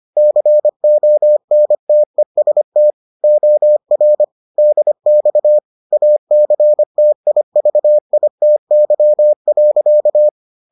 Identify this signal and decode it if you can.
Morse code, heard as a keyed tone, about 25 wpm, 600 Hz. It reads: CONTEST OR DX ACTIVITY.